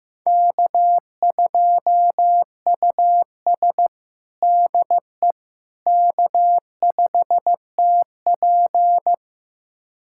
Morse code, heard as a keyed tone, about 15 wpm, 700 Hz. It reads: K2US DE K5TP
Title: USA Callsign DE Callsign - 4 Characters - 15wpm